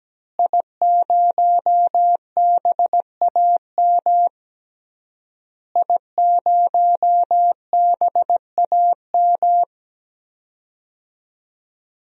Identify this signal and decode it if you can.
Morse code, heard as a keyed tone, about 17 wpm, 700 Hz. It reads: I0BAM I0BAM
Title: DXCC-100 Call Signs - 5 Characters - Repeated 3x - 17wpm